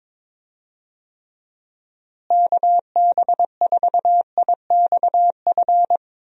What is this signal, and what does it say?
Morse code, heard as a keyed tone, about 22 wpm, 700 Hz. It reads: KB4IXF
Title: USA Call Signs - 6 Characters - Repeated 3x - 22wpm